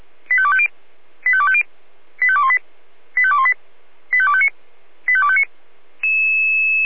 Five-tone selective call (ZVEI), heard as quick runs of five tones, tones of about 70 ms. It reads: 85250 85250 85218 85218 85259 85259
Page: /496/SDR/ZVEI/85253/